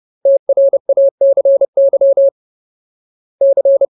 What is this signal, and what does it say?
Morse code, heard as a keyed tone, about 30 wpm, 550 Hz. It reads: TRACY C
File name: Ultimate Word List / 30WPM Ultimate